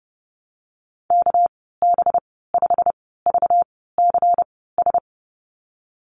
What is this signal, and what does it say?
Morse code, heard as a keyed tone, about 31 wpm, spaced as for 17 wpm, 700 Hz. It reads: KB5VCS
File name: Lesson 37g - Letter x - Call Signs 30wpm